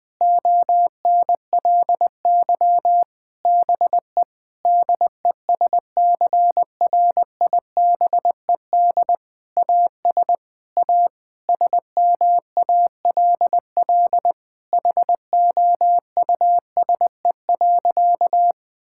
Morse code, heard as a keyed tone, about 20 wpm, 700 Hz. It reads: ONLY BE DESCRIBED AS A SMALL HOUSE.
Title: Sentences with 8-Word Phrases - 20wpm